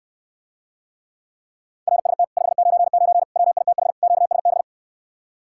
Morse code, heard as a keyed tone, about 69 wpm, 700 Hz. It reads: RST 599 WITH QSB